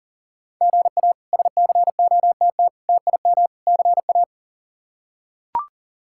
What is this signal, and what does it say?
Morse code, heard as a keyed tone, about 40 wpm, 700 Hz. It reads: GA SCOTT TIM CA
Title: K1USN SST QSO Practice - 40wpm